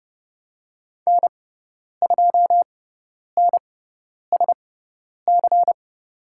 Morse code, heard as a keyed tone, about 30 wpm, spaced as for 10 wpm, 700 Hz. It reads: N2NSC
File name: Lesson 18g - Letter m - Call Signs 30wpm